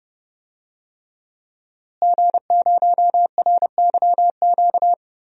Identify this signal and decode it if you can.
Morse code, heard as a keyed tone, about 28 wpm, 700 Hz. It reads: G0RYQ
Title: DXCC-100 Call Signs - 5 Characters - Repeated 3x - 30wpm